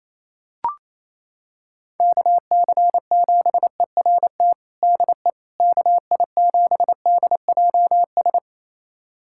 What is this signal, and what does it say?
Morse code, heard as a keyed tone, about 28 wpm, 700 Hz. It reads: KC7ERT DE KI7DJS
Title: USA Callsign DE Callsign - 6 Characters - 28wpm